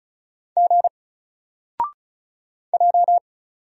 Morse code, heard as a keyed tone, about 36 wpm, 700 Hz. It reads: G J
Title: Single Letter-Number 35wpm